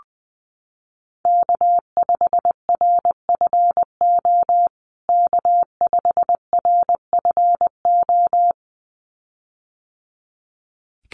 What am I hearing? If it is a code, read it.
Morse code, heard as a keyed tone, about 20 wpm, 700 Hz. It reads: K5RFO K5RFO
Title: USA Call Signs Repeated 2x 20wpm